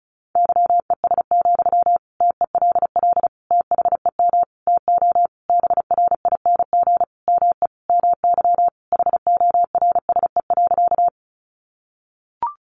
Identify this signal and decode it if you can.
Morse code, heard as a keyed tone, about 34 wpm, 700 Hz. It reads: YES, TELL THEM TO BRING ME MY HORSE.
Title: Sentences from Top 500 Words 35wpm